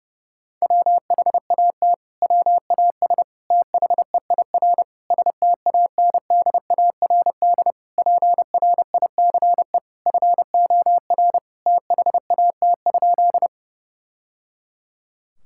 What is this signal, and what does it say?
Morse code, heard as a keyed tone, about 30 wpm, 700 Hz. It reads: WHAT WAS THEIR STANDARD PRICE FOR THAT?
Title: Sentences from Top 1000 Words - Review Part 6 - 30wpm